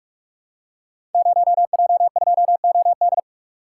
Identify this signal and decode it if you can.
Morse code, heard as a keyed tone, about 44 wpm, 700 Hz. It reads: 0J2OD
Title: Code Groups - 5 Letter:Numbers - 45wpm